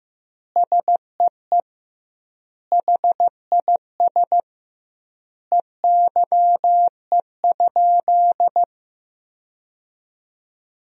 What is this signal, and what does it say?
Morse code, heard as a keyed tone, about 15 wpm, 700 Hz. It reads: SEE HIS EYE?
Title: 2x Word-Spacing - Sentences from Top 300 Words - 15wpm